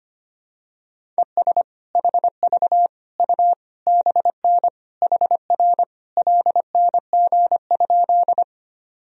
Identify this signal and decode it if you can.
Morse code, heard as a keyed tone, about 25 wpm, 700 Hz. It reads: ES HV U BN HR LNG?
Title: Sentences from Top 200 Words and Abbreviations 25wpm